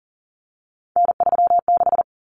Morse code, heard as a keyed tone, about 40 wpm, 700 Hz. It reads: N36
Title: Common Cut Numbers - 3-Digit 40wpm